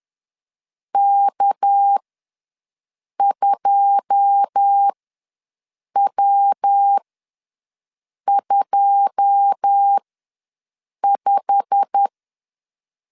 Morse code, heard as a keyed tone, about 10 wpm, spaced as for 6 wpm, 800 Hz. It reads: K2W25
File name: LEZIONI / 9 2 ws 10wpm 4130